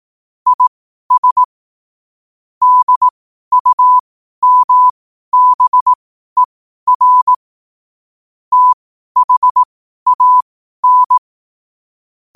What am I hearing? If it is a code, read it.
Morse code, heard as a keyed tone, about 18 wpm, spaced as for 12 wpm, 1000 Hz. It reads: IS DUMBER THAN
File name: qotdfiles / 12wpm